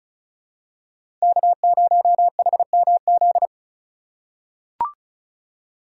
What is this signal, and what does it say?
Morse code, heard as a keyed tone, about 34 wpm, 700 Hz. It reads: K0HMZ
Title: K1USN SST QSO Practice - 35wpm